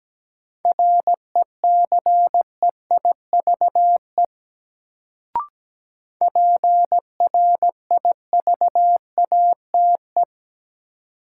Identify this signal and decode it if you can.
Morse code, heard as a keyed tone, about 17 wpm, 700 Hz. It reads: RECEIVE PRIVATE
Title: Sentences from Top 1000 Words - New Word Drill - 17wpm